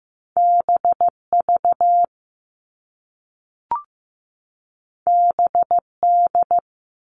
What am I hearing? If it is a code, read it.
Morse code, heard as a keyed tone, about 15 wpm, 700 Hz. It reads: BV BD
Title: All Cut Numbers - 2-Digit 15wpm